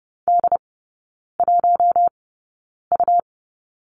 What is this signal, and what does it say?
Morse code, heard as a keyed tone, about 30 wpm, spaced as for 9 wpm, 700 Hz. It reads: D1U
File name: Lesson 17e - Letter u - Three Characters 30wpm